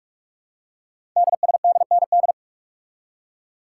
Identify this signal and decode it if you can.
Morse code, heard as a keyed tone, about 44 wpm, 700 Hz. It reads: DSDND
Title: Code Groups - 5 Letters - 45wpm